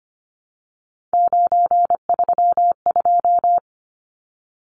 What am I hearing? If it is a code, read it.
Morse code, heard as a keyed tone, about 25 wpm, 690 Hz. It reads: 932
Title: Numbers - 3-Digit 25wpm